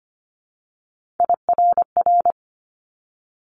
Morse code, heard as a keyed tone, about 25 wpm, 700 Hz. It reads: IRR
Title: Multi-letter Phonemes 25wpm